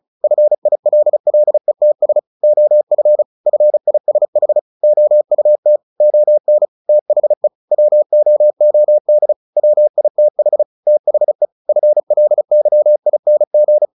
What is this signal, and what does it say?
Morse code, heard as a keyed tone, about 34 wpm, 600 Hz. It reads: FILLETS OF FISH OUT ON THE WOOD WITH THE FLYING
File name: the old man and the sea 2 35wpm